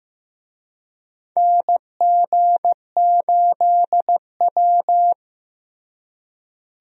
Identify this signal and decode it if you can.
Morse code, heard as a keyed tone, about 15 wpm, 700 Hz. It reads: NG8W
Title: USA Call Signs - 4 Characters - Repeated 2x - 15wpm